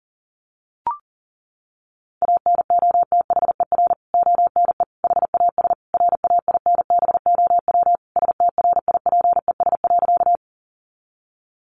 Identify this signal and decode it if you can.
Morse code, heard as a keyed tone, about 40 wpm, 700 Hz. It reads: ANOTHER ONE HAS RAINBOW STRIPES.
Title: Additional Easy Phrases 40wpm